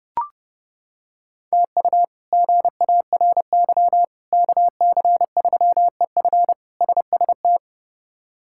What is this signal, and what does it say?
Morse code, heard as a keyed tone, about 30 wpm, 700 Hz. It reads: TU GARY KC3EF SST
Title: K1USN SST QSO Practice - 30wpm